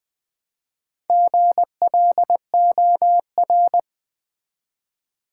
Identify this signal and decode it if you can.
Morse code, heard as a keyed tone, about 20 wpm, 700 Hz. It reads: GLOR
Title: Build A Word - 1 Letter At A Time - 5-Letter Words - 20wpm